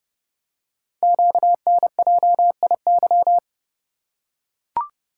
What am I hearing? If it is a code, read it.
Morse code, heard as a keyed tone, about 30 wpm, 690 Hz. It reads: QNJIY
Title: Code Groups - 5 Letters - 30wpm